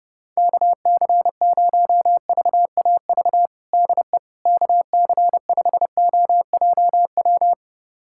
Morse code, heard as a keyed tone, about 32 wpm, 700 Hz. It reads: KC0VAV DE KC5OJW
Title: USA Callsign DE Callsign 30wpm